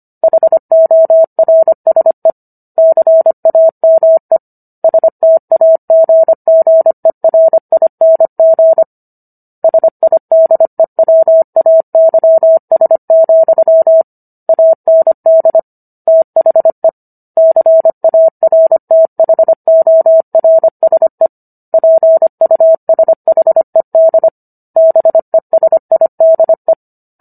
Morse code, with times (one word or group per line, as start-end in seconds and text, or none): 0.2-2.3 HORSE
2.8-4.4 CAME
4.8-8.8 STAGGERING
9.6-14.0 SIDEWAYS,
14.5-15.6 AND
16.1-16.9 THE
17.4-21.3 CARTHORSE
21.7-24.3 PUSHED
24.8-26.7 BESIDE